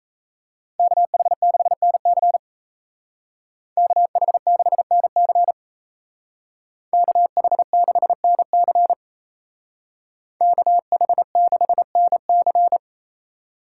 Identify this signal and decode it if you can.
Morse code, heard as a keyed tone, about 35 wpm, 700 Hz. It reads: KH6NC KH6NC KH6NC KH6NC
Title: Speed Racing - USA 2x2 Call Signs 28wpm